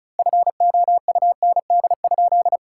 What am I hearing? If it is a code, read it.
Morse code, heard as a keyed tone, about 36 wpm, 700 Hz. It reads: FOUND?
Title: Sentences from Top 1600 Words 35wpm